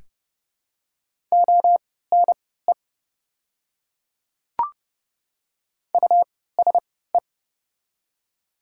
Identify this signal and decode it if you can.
Morse code, heard as a keyed tone, about 30 wpm, spaced as for 17 wpm, 700 Hz. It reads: ONE USE
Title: 3x Character Spacing - Top 100 Words - 30wpm